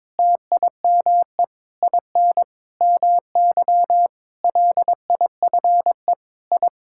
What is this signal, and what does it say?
Morse code, heard as a keyed tone, about 22 wpm, 700 Hz. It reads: TIME IN MY LIFE I
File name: Sentences with 7-Word Phrases - 22wpm